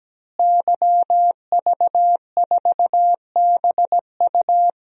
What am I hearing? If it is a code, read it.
Morse code, heard as a keyed tone, about 17 wpm, 700 Hz. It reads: YV4BU
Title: DXCC-100 Call Signs - 5 Characters - Repeated 3x - 17wpm